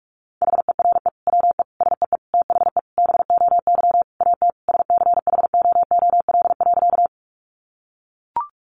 Morse code, heard as a keyed tone, about 44 wpm, 700 Hz. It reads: HERE WE SEE THE BOY AT SCHOOL.
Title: Sentences from Top 300 Words 45wpm